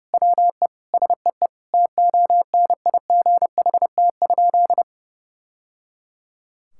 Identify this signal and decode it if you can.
Morse code, heard as a keyed tone, about 30 wpm, 700 Hz. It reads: WE SEE TONIGHT?
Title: Sentences from Top 1000 Words - Review Part 5 - 30wpm